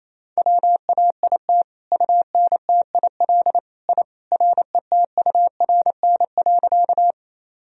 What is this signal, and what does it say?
Morse code, heard as a keyed tone, about 28 wpm, 700 Hz. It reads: WAIT UNTIL I RETURN.
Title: Sentences from Top 1000 Words - Review Part 1 - 28wpm